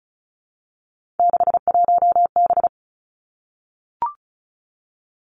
Morse code, heard as a keyed tone, about 36 wpm, 700 Hz. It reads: 61B
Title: Three Challenging Letter-Number Combinations 35wpm